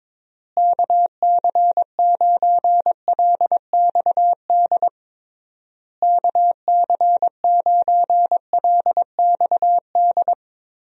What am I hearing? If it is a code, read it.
Morse code, heard as a keyed tone, about 22 wpm, 700 Hz. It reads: KC9LXD KC9LXD
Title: USA Call Signs - 6 Characters - Repeated 3x - 22wpm